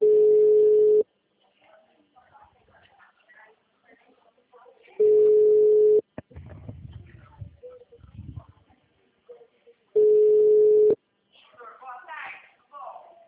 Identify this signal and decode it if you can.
Ringback tone (Europe, 425 Hz), with speech.